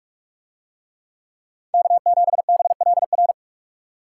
Morse code, heard as a keyed tone, about 45 wpm, 700 Hz. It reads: K7BLR